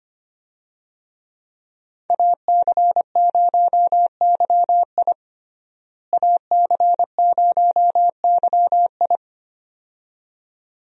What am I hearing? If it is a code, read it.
Morse code, heard as a keyed tone, about 25 wpm, 700 Hz. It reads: AC0YI AC0YI
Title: USA Call Signs - 5 Characters - Repeated 2x - 25wpm